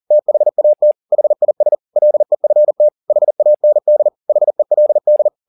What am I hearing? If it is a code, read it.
Morse code, heard as a keyed tone, about 40 wpm, 600 Hz. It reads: THAT HIS LEFT HAND HELD
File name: the old man and the sea 2 40wpm